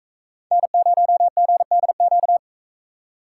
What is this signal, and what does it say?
Morse code, heard as a keyed tone, about 44 wpm, 700 Hz. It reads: N0GDQ